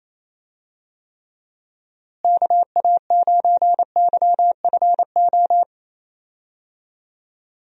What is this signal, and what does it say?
Morse code, heard as a keyed tone, about 28 wpm, 700 Hz. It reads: KA9YFO
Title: USA Call Signs - 6 Characters - Repeated 2x - 28wpm